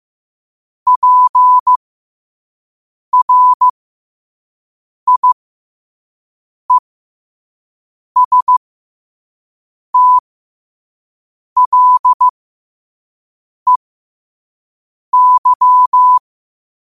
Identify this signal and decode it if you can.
Morse code, heard as a keyed tone, about 15 wpm, spaced as for 5 wpm, 1000 Hz. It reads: PRIESTLEY
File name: qotdfiles / 5wpm